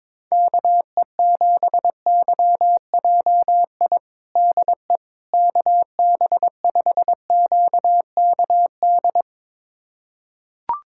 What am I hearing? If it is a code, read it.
Morse code, heard as a keyed tone, about 22 wpm, 700 Hz. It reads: KE7YJI DE KB5QKD
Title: USA Callsign DE Callsign - 6 Characters - 22wpm